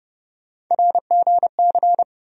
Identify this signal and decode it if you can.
Morse code, heard as a keyed tone, about 32 wpm, 700 Hz. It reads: RGC
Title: USA Call Sign Postfixes - 3 Letters 30wpm